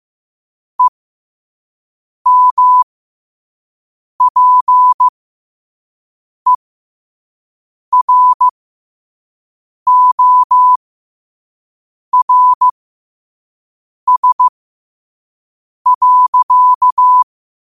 Morse code, heard as a keyed tone, about 15 wpm, spaced as for 5 wpm, 1000 Hz. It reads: EMPERORS.
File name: qotdfiles / 5wpm